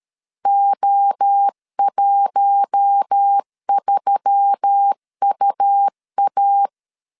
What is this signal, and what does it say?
Morse code, heard as a keyed tone, about 13 wpm, 800 Hz. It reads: O13UA
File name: LEZIONI / 6 PKX0 / 6 PKX 12wpm 2024